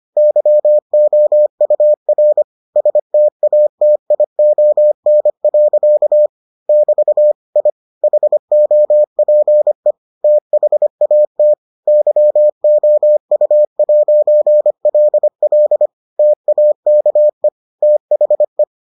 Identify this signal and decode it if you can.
Morse code, heard as a keyed tone, about 25 wpm, 600 Hz. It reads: YOUR STATION. <BT> I HOPE THAT YOU'LL TAKE THE